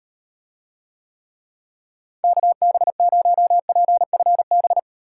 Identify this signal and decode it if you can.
Morse code, heard as a keyed tone, about 38 wpm, 700 Hz. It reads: KB0PFB